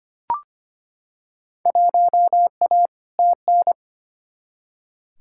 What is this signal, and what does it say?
Morse code, heard as a keyed tone, about 25 wpm, 700 Hz. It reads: 1A TN
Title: Field Day Exchanges - 25wpm